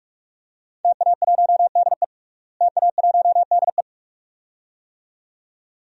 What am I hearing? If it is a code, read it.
Morse code, heard as a keyed tone, about 47 wpm, 700 Hz. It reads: TA1DE TA1DE
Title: DXCC-100 Call Signs - 5 Characters - Repeated 2x - 45wpm